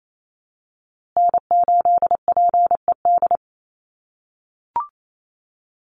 Morse code, heard as a keyed tone, about 28 wpm, 700 Hz. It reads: N8PED
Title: USA Call Signs Repeated 2x 28wpm